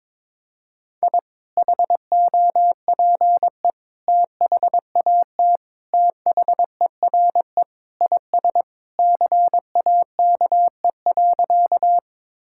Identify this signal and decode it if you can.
Morse code, heard as a keyed tone, about 22 wpm, 700 Hz. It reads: I HOPE THAT THERE IS CAKE.
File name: practice / 22wpm / Sentences with 2-Word Phrases - 22wpm